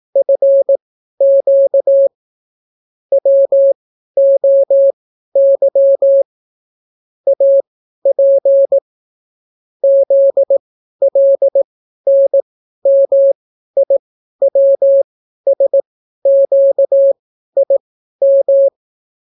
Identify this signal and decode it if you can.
Morse code, heard as a keyed tone, about 18 wpm, spaced as for 12 wpm, 550 Hz. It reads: FQ WOY AP ZLNMIWSQIM